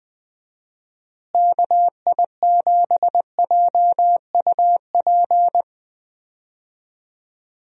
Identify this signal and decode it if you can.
Morse code, heard as a keyed tone, about 21 wpm, 700 Hz. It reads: KI7JUP
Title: USA Call Signs - 6 Characters - Repeated 2x - 20wpm